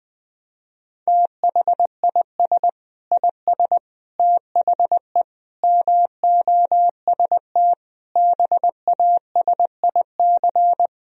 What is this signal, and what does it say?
Morse code, heard as a keyed tone, about 20 wpm, 700 Hz. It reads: THIS IS THE MOST BASIC
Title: Sentences from Top 1000 Words - Review Part 3 - 20wpm